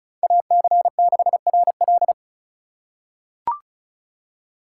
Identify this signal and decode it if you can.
Morse code, heard as a keyed tone, about 36 wpm, 700 Hz. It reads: AC6RL
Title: USA Call Signs - 5 Characters - 4x Character Spacing - 35wpm